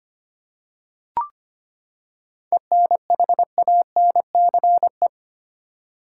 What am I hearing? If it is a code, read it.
Morse code, heard as a keyed tone, about 25 wpm, 700 Hz. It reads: ENHANCE
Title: Seven Letter Words - 25wpm